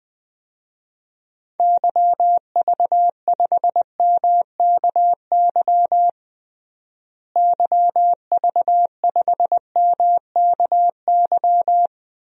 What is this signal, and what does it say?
Morse code, heard as a keyed tone, about 20 wpm, 700 Hz. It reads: YV5MKY YV5MKY
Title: DXCC-100 Call Signs - 6 Characters - Repeated 3x - 20wpm